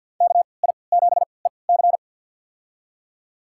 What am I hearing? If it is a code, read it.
Morse code, heard as a keyed tone, about 51 wpm, spaced as for 28 wpm, 700 Hz. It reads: KI7EC
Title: USA Call Signs - 5 Characters - 3x Character Spacing - 50wpm